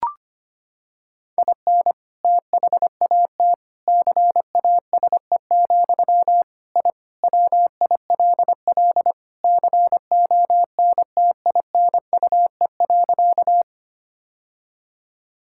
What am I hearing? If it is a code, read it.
Morse code, heard as a keyed tone, about 25 wpm, 700 Hz. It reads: IN THAT CASE, I WILL CONTINUE.
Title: Sentences from Top 1000 Words - Review Part 6 - 25wpm